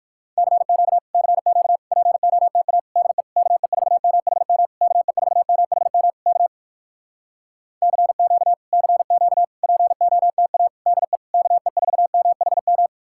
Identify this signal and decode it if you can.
Morse code, heard as a keyed tone, about 57 wpm, 700 Hz. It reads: CQ CQ POTA DE KE4MHM KE4MHM K CQ CQ POTA DE KE4MHM